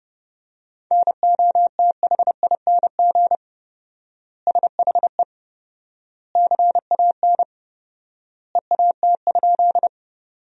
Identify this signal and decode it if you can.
Morse code, heard as a keyed tone, about 30 wpm, 700 Hz. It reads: NOTHING SHE CAN EAT?
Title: 4x Word-Spacing - Sentences from Top 500 Words - 30wpm